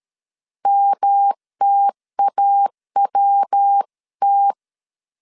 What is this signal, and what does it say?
Morse code, heard as a keyed tone, about 13 wpm, 800 Hz. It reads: MTAWT
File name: CW / DET / 1 3 1500C S C 12wpm 2750